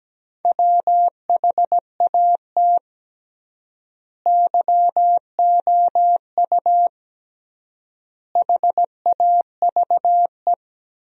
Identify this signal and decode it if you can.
Morse code, heard as a keyed tone, about 17 wpm, 700 Hz. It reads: WHAT YOU HAVE
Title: 3x Word-Spacing - Sentences from Top 200 Words - 17wpm